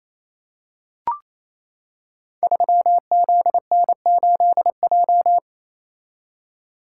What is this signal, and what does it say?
Morse code, heard as a keyed tone, about 28 wpm, 700 Hz. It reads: 3ZN8J
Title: Code Groups - 5 Letter:Numbers - 28wpm